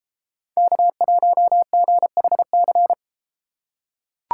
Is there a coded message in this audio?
Yes: Morse code from 0.6 to 2.9 s, heard as a keyed tone, about 32 wpm, 700 Hz. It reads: K1GHC